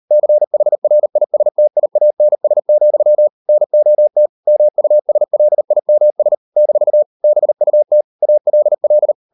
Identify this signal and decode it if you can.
Morse code, heard as a keyed tone, about 38 wpm, 600 Hz. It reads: CHRISTIANS, NOT MUSLIMS - BUT ALL